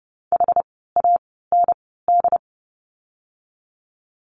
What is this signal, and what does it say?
Morse code, heard as a keyed tone, about 30 wpm, spaced as for 17 wpm, 700 Hz. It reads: HAND
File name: Lesson 17f - Letter u - Words 30wpm